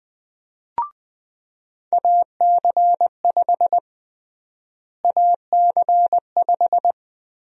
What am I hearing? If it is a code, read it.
Morse code, heard as a keyed tone, about 20 wpm, 700 Hz. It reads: AC5 AC5
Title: USA Call Sign Prefixes - Repeated 2x - 20wpm